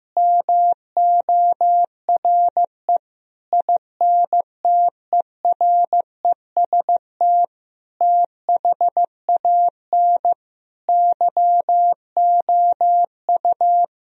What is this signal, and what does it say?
Morse code, heard as a keyed tone, about 15 wpm, 700 Hz. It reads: MORE INTEREST THAN YOU
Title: Sentences from Top 1000 Words - Review Part 2 - 15wpm